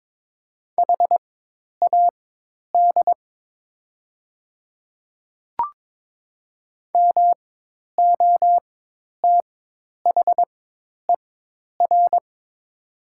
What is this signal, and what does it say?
Morse code, heard as a keyed tone, about 22 wpm, spaced as for 10 wpm, 700 Hz. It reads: HAD MOTHER
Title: 4x Character Spacing - Top 300 Words - 22wpm